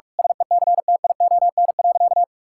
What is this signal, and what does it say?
Morse code, heard as a keyed tone, about 46 wpm, 700 Hz. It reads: SECTION.
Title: Sentences from Top 1000 Words - Review Part 5 - 45wpm